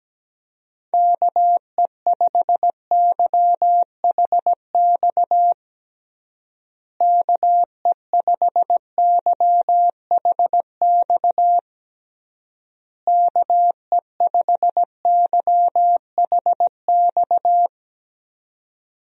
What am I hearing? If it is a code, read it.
Morse code, heard as a keyed tone, about 17 wpm, 700 Hz. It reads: KE5YHX KE5YHX KE5YHX
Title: USA Call Signs - 6 Characters - Repeated 3x - 17wpm